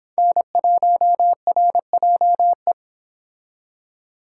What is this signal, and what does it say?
Morse code, heard as a keyed tone, about 25 wpm, 700 Hz. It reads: N1RJE